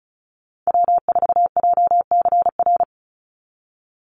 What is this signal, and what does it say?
Morse code, heard as a keyed tone, about 36 wpm, 700 Hz. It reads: W4JCR
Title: USA Call Signs 35wpm